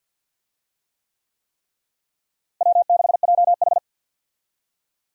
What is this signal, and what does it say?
Morse code, heard as a keyed tone, about 48 wpm, 700 Hz. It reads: W6JH
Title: USA Call Signs - 4 Characters - Repeated 3x - 50wpm